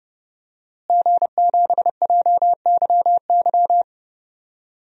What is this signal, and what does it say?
Morse code, heard as a keyed tone, about 30 wpm, 700 Hz. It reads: G7JYY
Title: DXCC-100 Call Signs - 5 Characters - Repeated 2x - 30wpm